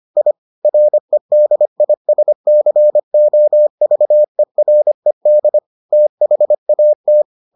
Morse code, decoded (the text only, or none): I REDISCOVERED THAT